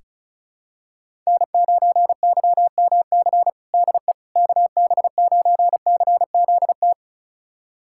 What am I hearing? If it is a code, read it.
Morse code, heard as a keyed tone, about 36 wpm, 700 Hz. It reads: N9YMC DE KB9CZT
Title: USA Callsign DE Callsign 35wpm